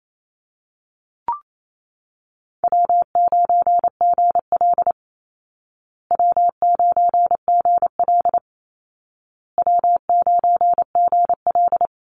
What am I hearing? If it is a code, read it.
Morse code, heard as a keyed tone, about 28 wpm, 700 Hz. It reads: W9GL W9GL W9GL
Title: Easy Call Signs Repeated 3x - 28wpm